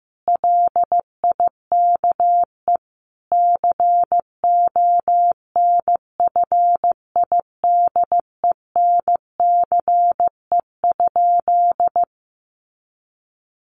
Morse code, heard as a keyed tone, about 15 wpm, 700 Hz. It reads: LIKE CONFIDENCE?
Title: Sentences from Top 1500 Words 15wpm